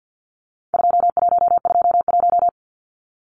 Morse code, heard as a keyed tone, about 50 wpm, 700 Hz. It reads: 2121
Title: Numbers - 4-Digit 50wpm